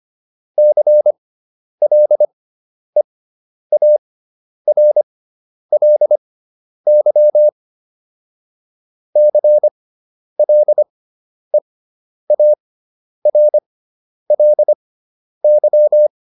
Morse code, heard as a keyed tone, about 25 wpm, spaced as for 10 wpm, 600 Hz. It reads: CLEARLY CLEARLY
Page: fundamental-practice-files